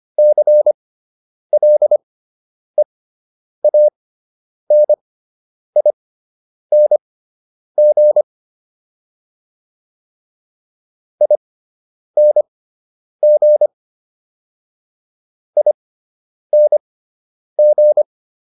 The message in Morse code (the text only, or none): CLEANING ING ING